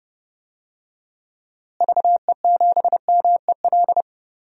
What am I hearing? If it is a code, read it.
Morse code, heard as a keyed tone, about 30 wpm, 700 Hz. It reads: VE7MEL
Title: DXCC-100 Call Signs - 6 Characters - Repeated 3x - 30wpm